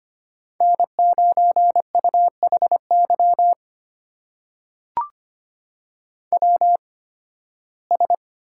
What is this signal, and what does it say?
Morse code, heard as a keyed tone, about 25 wpm, spaced as for 16 wpm, 700 Hz. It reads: N9UHY WS
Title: USA Call Signs - 5 Characters - 8x Character Spacing - 25wpm